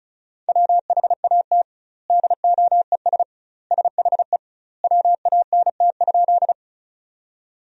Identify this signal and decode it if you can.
Morse code, heard as a keyed tone, about 36 wpm, 700 Hz. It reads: WHAT DOES SHE WANT?
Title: 2x Word-Spacing - Sentences from Top 200 Words - 35wpm